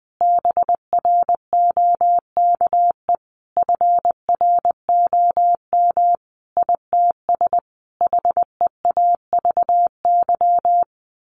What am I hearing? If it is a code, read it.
Morse code, heard as a keyed tone, about 20 wpm, 700 Hz. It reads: BROKE FROM ITS HEAVY
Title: A Sound Of Thunder 20wpm